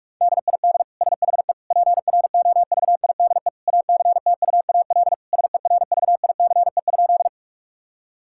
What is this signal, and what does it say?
Morse code, heard as a keyed tone, about 46 wpm, 700 Hz. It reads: DID SHE PROVIDE ACTUAL SERVICE?
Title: Sentences from Top 1000 Words - Review Part 5 - 45wpm